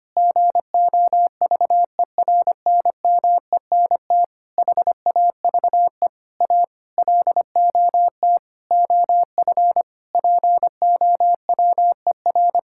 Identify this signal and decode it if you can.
Morse code, heard as a keyed tone, about 25 wpm, 700 Hz. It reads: GOVERNMENT HAVE A LOT OF POWER
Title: Sentences from Top 1000 Words - Review Part 5 - 25wpm